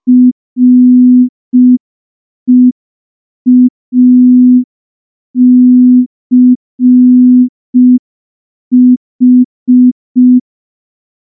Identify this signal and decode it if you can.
Morse code, heard as a keyed tone, about 5 wpm, 250 Hz. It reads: REACH
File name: morse / morse-training-05WPM